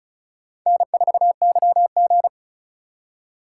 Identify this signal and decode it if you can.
Morse code, heard as a keyed tone, about 34 wpm, 700 Hz. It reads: N4YG
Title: USA Call Signs - 3x Character Spacing - 35wpm